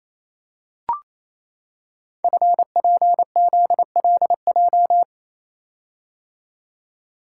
Morse code, heard as a keyed tone, about 28 wpm, 700 Hz. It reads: FPZLJ
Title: Code Groups - 5 Letter:Numbers - 28wpm